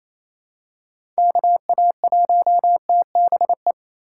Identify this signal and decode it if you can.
Morse code, heard as a keyed tone, about 28 wpm, 700 Hz. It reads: KA1TBE
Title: USA Call Signs - 6x Character Spacing - 28wpm